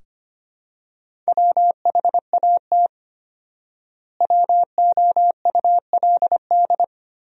Morse code, heard as a keyed tone, about 25 wpm, 700 Hz. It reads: WHAT WOULD
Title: 4x Word-Spacing - Sentences from Top 300 Words - 25wpm